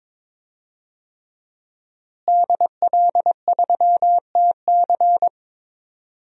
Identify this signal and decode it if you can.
Morse code, heard as a keyed tone, about 21 wpm, 700 Hz. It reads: DL3TC
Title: DXCC-100 Call Signs - 5 Characters - Repeated 3x - 22wpm